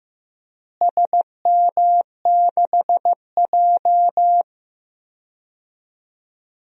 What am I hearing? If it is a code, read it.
Morse code, heard as a keyed tone, about 15 wpm, 700 Hz. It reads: SM6J
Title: DXCC-100 Call Signs - 4 Characters - Repeated 3x - 15wpm